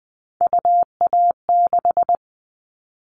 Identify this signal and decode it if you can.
Morse code, heard as a keyed tone, about 20 wpm, 700 Hz. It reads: UA6
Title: Callsign Prefixes for Top 50 DXCC - 20wpm